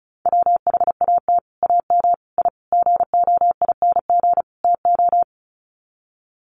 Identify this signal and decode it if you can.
Morse code, heard as a keyed tone, about 34 wpm, 700 Hz. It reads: WHAT AM I GOING TO
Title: Sets of 5 Words Encore 35wpm